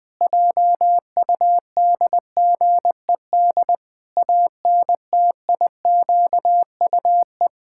Morse code, heard as a keyed tone, about 21 wpm, 700 Hz. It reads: JUDGED ANTIQUE
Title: Pangram Challenge v2 - 20wpm